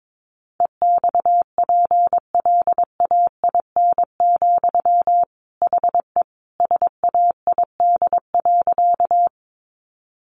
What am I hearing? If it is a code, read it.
Morse code, heard as a keyed tone, about 22 wpm, 700 Hz. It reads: EXPLAIN, HE SAID.
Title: Sentences from Top 1100 Words 22wpm